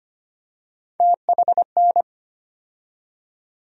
Morse code, heard as a keyed tone, about 25 wpm, 700 Hz. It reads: THN
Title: USA Call Sign Postfixes - 3 Letters 25wpm